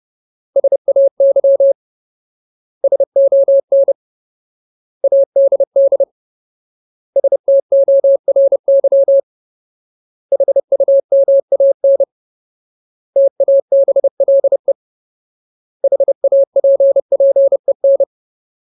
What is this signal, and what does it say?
Morse code, heard as a keyed tone, about 30 wpm, 550 Hz. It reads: SAY SON ADD STORY HUMAN TABLE HAPPEN
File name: Ultimate Word List / 30WPM Ultimate